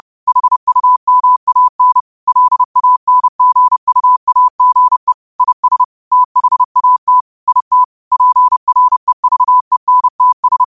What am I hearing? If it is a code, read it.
Morse code, heard as a keyed tone, about 30 wpm, 1000 Hz. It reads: HUMAN LANGUAGE IS THAT IT PREVENTS